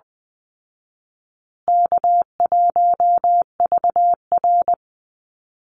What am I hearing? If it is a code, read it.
Morse code, heard as a keyed tone, about 20 wpm, 700 Hz. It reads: K1VR
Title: Easy Call Signs Repeated 2x - 20wpm